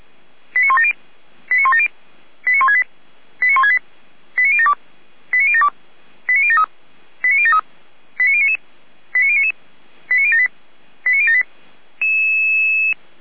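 Five-tone selective call (ZVEI), heard as quick runs of five tones, tones of about 70 ms. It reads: 89170 89170 89168 89168 89072 89072 89073 89073 89099 89099 89078 89078
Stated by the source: ZVEI five-tone selective call 89560 received on an SDR.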